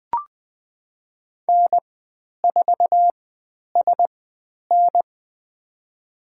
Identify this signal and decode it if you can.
Morse code, heard as a keyed tone, about 20 wpm, spaced as for 10 wpm, 700 Hz. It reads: N4SN
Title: 109 Lesson 18g - Letter m - Call Signs 20wpm